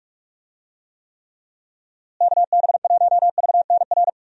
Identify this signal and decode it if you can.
Morse code, heard as a keyed tone, about 46 wpm, 700 Hz. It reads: KB1VNR
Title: USA Call Signs - 6 Characters - Repeated 3x - 45wpm